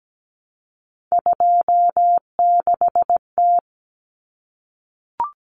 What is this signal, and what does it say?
Morse code, heard as a keyed tone, about 17 wpm, 700 Hz. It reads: 26T
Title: Common Cut Numbers - 3-Digit 17wpm